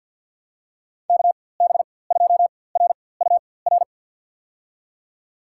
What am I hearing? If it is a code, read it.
Morse code, heard as a keyed tone, about 51 wpm, spaced as for 23 wpm, 700 Hz. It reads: KB2RUR
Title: USA Call Signs - 6 Characters - 4x Character Spacing - 50wpm